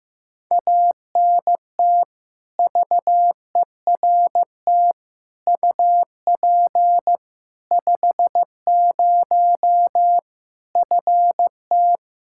Morse code, heard as a keyed tone, about 15 wpm, 700 Hz. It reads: ANT VERT UP 50 FT
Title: Practice QSO - Exchange 2 - Info - 15wpm